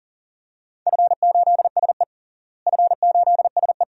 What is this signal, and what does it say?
Morse code, heard as a keyed tone, about 40 wpm, 700 Hz. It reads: F8SE F8SE
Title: DXCC-100 Call Signs - 4 Characters - Repeated 2x - 40wpm